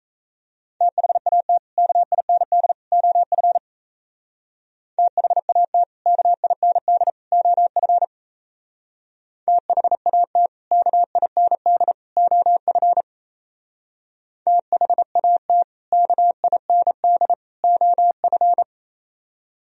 Morse code, heard as a keyed tone, about 33 wpm, 700 Hz. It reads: THAT KIND OF THAT KIND OF THAT KIND OF THAT KIND OF
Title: Speed-Racing - Sets of 3 Words 28wpm